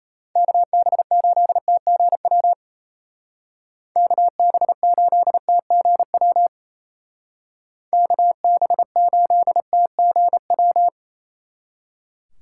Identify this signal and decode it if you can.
Morse code, heard as a keyed tone, about 35 wpm, 700 Hz. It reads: KB8TGW KB8TGW KB8TGW
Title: Speed Racing - USA 2x3 Call Signs 28wpm